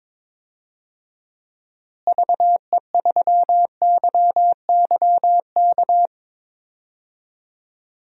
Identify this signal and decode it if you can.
Morse code, heard as a keyed tone, about 22 wpm, 700 Hz. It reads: VE3YYK
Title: DXCC-100 Call Signs - 6 Characters - Repeated 3x - 22wpm